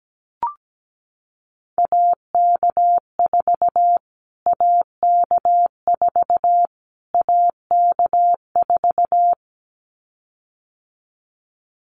Morse code, heard as a keyed tone, about 17 wpm, 700 Hz. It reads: AK4 AK4 AK4
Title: Callsign Prefixes for Top 25 DXCC - Repeated 3x - 17wpm